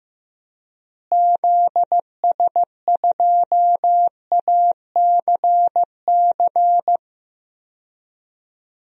Morse code, heard as a keyed tone, about 16 wpm, 700 Hz. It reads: ZS2ACC
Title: DXCC-100 Call Signs - 6 Characters - Repeated 3x - 15wpm